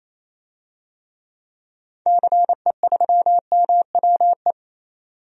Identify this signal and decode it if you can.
Morse code, heard as a keyed tone, about 28 wpm, 700 Hz. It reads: CE3MWE
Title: DXCC-100 Call Signs - 6 Characters - 28wpm